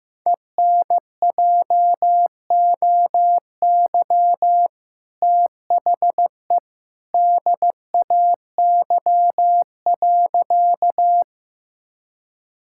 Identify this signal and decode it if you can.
Morse code, heard as a keyed tone, about 15 wpm, 700 Hz. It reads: ENJOY THE DAY.
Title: Sentences from Top 1000 Words - Review Part 5 - 15wpm